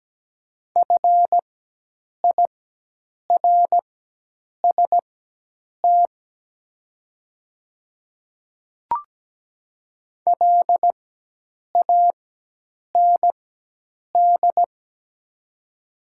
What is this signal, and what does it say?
Morse code, heard as a keyed tone, about 17 wpm, spaced as for 8 wpm, 700 Hz. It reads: FIRST LAND
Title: 4x Character Spacing - Top 300 Words - 17wpm